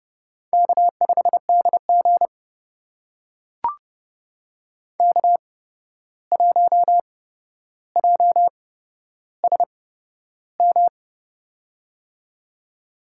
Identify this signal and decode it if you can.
Morse code, heard as a keyed tone, about 30 wpm, spaced as for 12 wpm, 700 Hz. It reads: K5DG K1JSM
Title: USA Call Signs - 8x Character Spacing - 30wpm